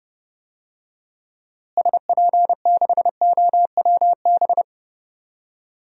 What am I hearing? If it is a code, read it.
Morse code, heard as a keyed tone, about 30 wpm, 700 Hz. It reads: SP6OWB
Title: DXCC-100 Call Signs - 6 Characters - Repeated 2x - 30wpm